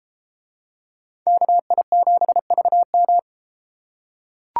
Morse code, heard as a keyed tone, about 32 wpm, 700 Hz. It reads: KI7VM